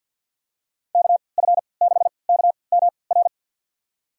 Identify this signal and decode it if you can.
Morse code, heard as a keyed tone, about 48 wpm, spaced as for 28 wpm, 700 Hz. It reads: KF6KMR
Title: USA Call Signs - 6 Characters - 3x Character Spacing - 50wpm